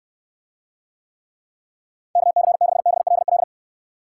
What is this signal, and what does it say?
Morse code, heard as a keyed tone, about 69 wpm, 700 Hz. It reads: KC6BLL